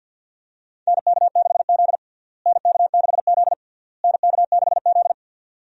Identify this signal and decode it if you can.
Morse code, heard as a keyed tone, about 50 wpm, 700 Hz. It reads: NK6Z NK6Z NK6Z
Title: USA Call Signs - 4 Characters - Repeated 3x - 50wpm